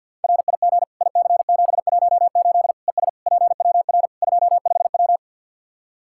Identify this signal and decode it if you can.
Morse code, heard as a keyed tone, about 52 wpm, 700 Hz. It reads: RIG IC718 ES PWR 25W
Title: Practice QSO - Exchange 2 - Info - 50wpm